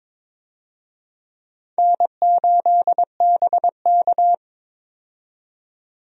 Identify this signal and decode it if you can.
Morse code, heard as a keyed tone, about 22 wpm, 700 Hz. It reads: N8BK
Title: USA Call Signs - 4 Characters - 22wpm